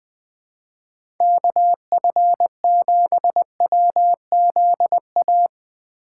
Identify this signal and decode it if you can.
Morse code, heard as a keyed tone, about 20 wpm, 700 Hz. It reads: KF7WZA
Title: USA Call Signs - 4x Character Spacing - 20wpm